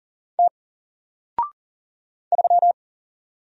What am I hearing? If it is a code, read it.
Morse code, heard as a keyed tone, about 40 wpm, 700 Hz. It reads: T 3
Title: Single Letter-Number Alt 40wpm